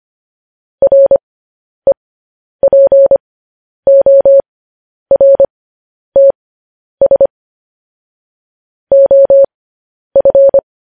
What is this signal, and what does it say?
Morse code, heard as a keyed tone, about 25 wpm, spaced as for 10 wpm, 550 Hz. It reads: REPORTS OF
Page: News Headlines in Morse Code at 10 WPM